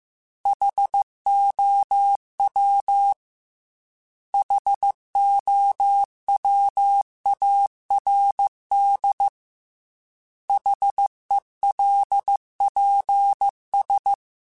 Morse code, heard as a keyed tone, about 15 wpm, 800 Hz. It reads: HOW HOWARD HELPS